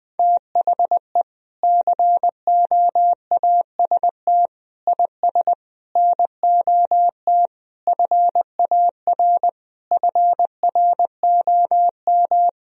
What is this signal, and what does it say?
Morse code, heard as a keyed tone, about 20 wpm, 700 Hz. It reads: THE COAST IS NOT FAR FROM
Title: Sentences from Top 1000 Words - Review Part 4 - 20wpm